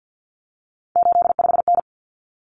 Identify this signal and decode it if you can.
Morse code, heard as a keyed tone, about 50 wpm, 700 Hz. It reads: 85N
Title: Common Cut Numbers - 3-Digit 50wpm